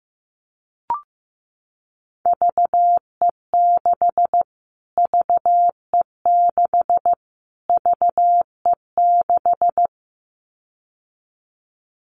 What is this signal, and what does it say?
Morse code, heard as a keyed tone, about 15 wpm, 700 Hz. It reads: VE6 VE6 VE6
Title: Callsign Prefixes for Top 25 DXCC - Repeated 3x - 15wpm